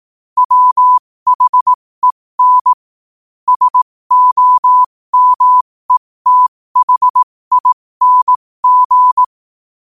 Morse code, heard as a keyed tone, about 18 wpm, spaced as for 15 wpm, 1000 Hz. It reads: WHEN SOMETHING
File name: qotdfiles / 15wpm